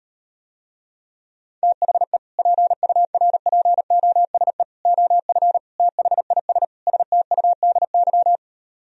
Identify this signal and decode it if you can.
Morse code, heard as a keyed tone, about 38 wpm, 700 Hz. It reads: THE PURPOSE OF THIS STUDY